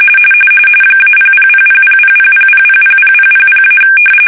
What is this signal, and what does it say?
RTTY, about 45.45 baud. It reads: -000000000000000-
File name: fsk-ascii850-pix / fish